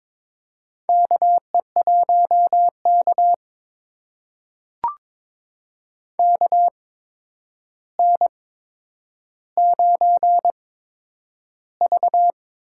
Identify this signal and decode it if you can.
Morse code, heard as a keyed tone, about 22 wpm, spaced as for 9 wpm, 700 Hz. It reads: KE1K KN9V
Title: USA Call Signs - 4 Characters - 8x Character Spacing - 22wpm